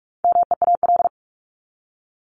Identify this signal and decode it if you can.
Morse code, heard as a keyed tone, about 44 wpm, 700 Hz. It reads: MEAL
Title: Top 1751-2000 Words 45wpm